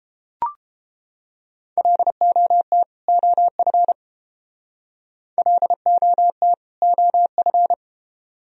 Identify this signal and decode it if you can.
Morse code, heard as a keyed tone, about 30 wpm, 700 Hz. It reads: LOT OF LOT OF